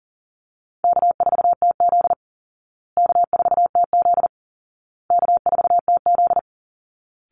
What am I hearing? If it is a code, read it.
Morse code, heard as a keyed tone, about 38 wpm, 700 Hz. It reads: K4TZ K4TZ K4TZ
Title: Easy Call Signs Repeated 3x - 40wpm